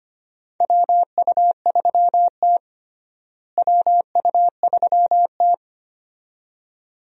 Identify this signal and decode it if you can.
Morse code, heard as a keyed tone, about 25 wpm, 700 Hz. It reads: WU3T WU3T
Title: USA Call Signs - 4 Characters - Repeated 2x - 25wpm